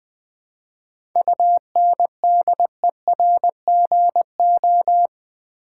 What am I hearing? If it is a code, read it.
Morse code, heard as a keyed tone, about 20 wpm, 690 Hz. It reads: UNDERGO
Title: 7 Letter Words - First 3 Letters Given - 20wpm